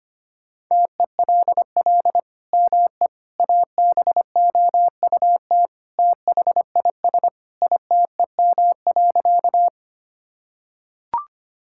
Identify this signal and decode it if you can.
Morse code, heard as a keyed tone, about 25 wpm, 700 Hz. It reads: TELL ME ABOUT THIS ITEM.
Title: Sentences from Top 1000 Words - Review Part 6 - 25wpm